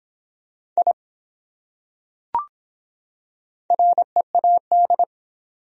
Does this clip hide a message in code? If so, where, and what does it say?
Yes: Morse code from 0.8 to 5.0 s, heard as a keyed tone, about 22 wpm, 700 Hz. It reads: I READ